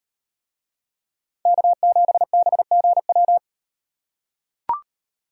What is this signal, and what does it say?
Morse code, heard as a keyed tone, about 36 wpm, 700 Hz. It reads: K7BGW